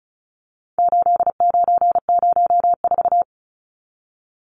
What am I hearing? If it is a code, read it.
Morse code, heard as a keyed tone, about 34 wpm, 700 Hz. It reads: 8904
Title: Numbers - 4-Digit 35wpm